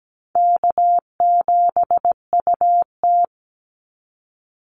Morse code, heard as a keyed tone, about 17 wpm, 700 Hz. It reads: K7UT
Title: Easy Call Signs Repeated 3x - 17wpm